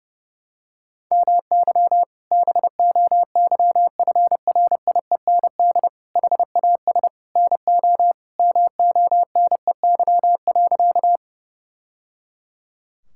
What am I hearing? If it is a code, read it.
Morse code, heard as a keyed tone, about 30 wpm, 700 Hz. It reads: MY BOYFRIEND HAS NO MONEY.
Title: Sentences from Top 1000 Words - Review Part 5 - 30wpm